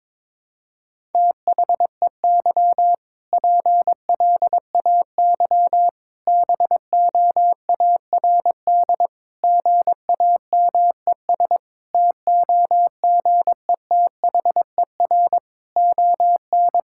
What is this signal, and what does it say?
Morse code, heard as a keyed tone, about 22 wpm, 700 Hz. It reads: THEY PLAY BOARD GAMES TOGETHER ON
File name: Difficult Sentences with Hints - 22wpm